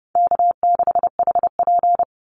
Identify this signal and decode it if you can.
Morse code, heard as a keyed tone, about 30 wpm, 700 Hz. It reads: K6HP
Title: Call Signs Repeated 2x - 30wpm